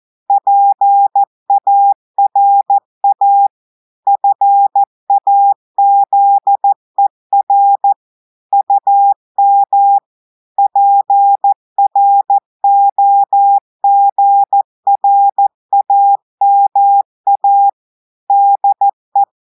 Morse code, heard as a keyed tone, about 14 wpm, 800 Hz. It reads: PARA FAZER UM PROGRAMA DE
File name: audios / portugais / portugais texte en clair 2 vitesse 14 mots minutes